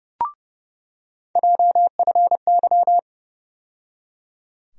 Morse code, heard as a keyed tone, about 30 wpm, 700 Hz. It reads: JFY
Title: USA Call Sign Postfixes - 3 Letters 30wpm